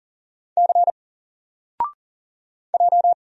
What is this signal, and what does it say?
Morse code, heard as a keyed tone, about 40 wpm, 700 Hz. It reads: C J
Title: Single Letter-Number 40wpm